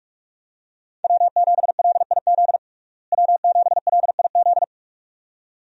Morse code, heard as a keyed tone, about 46 wpm, 700 Hz. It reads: W7LIZ W7LIZ
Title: USA Call Signs - 5 Characters - Repeated 2x - 45wpm